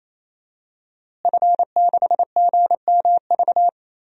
Morse code, heard as a keyed tone, about 28 wpm, 700 Hz. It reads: F6GMV
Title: DXCC-100 Call Signs - 5 Characters - Repeated 3x - 28wpm